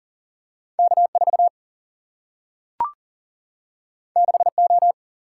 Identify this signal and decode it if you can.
Morse code, heard as a keyed tone, about 40 wpm, 700 Hz. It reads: K 4 6 O
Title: Single-Letter Single-Number 40wpm